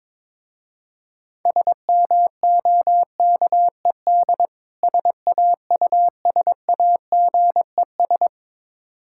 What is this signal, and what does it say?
Morse code, heard as a keyed tone, about 22 wpm, 690 Hz. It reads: SMOKED SAUSAGES
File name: Difficult Words with Hints - 22wpm